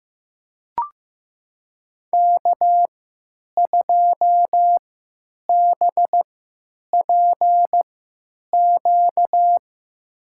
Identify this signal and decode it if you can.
Morse code, heard as a keyed tone, about 15 wpm, spaced as for 9 wpm, 700 Hz. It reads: K2BPQ
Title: USA Call Signs - 5 Characters - 3x Character Spacing - 15wpm